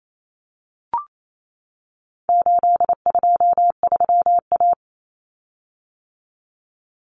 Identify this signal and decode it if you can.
Morse code, heard as a keyed tone, about 28 wpm, 700 Hz. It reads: 823A
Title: Common Cut Numbers - 4-Digit 28wpm